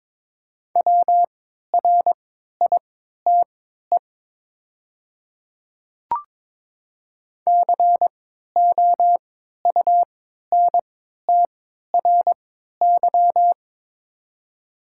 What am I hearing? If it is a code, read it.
Morse code, heard as a keyed tone, about 22 wpm, spaced as for 12 wpm, 700 Hz. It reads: WRITE COUNTRY
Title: 3x Character Spacing - Top 500 Words - 22wpm